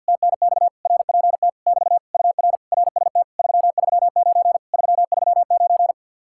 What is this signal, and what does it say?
Morse code, heard as a keyed tone, about 48 wpm, 700 Hz. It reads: TNX RPT <BT> UR RST 339 339
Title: Speed-Racing Common QSO Phrases - 50wpm